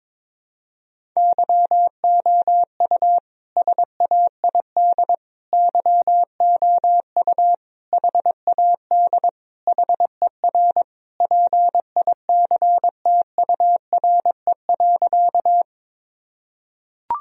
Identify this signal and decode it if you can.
Morse code, heard as a keyed tone, about 22 wpm, 700 Hz. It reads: YOU SAID YOU HAD HER PICTURE.
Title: Sentences from Top 1000 Words - Review Part 1 - 22wpm